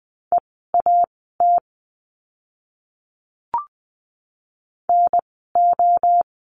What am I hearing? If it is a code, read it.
Morse code, heard as a keyed tone, about 20 wpm, spaced as for 14 wpm, 700 Hz. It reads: EAT NO
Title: 068 Lesson 12f - Letter d - Words 20wpm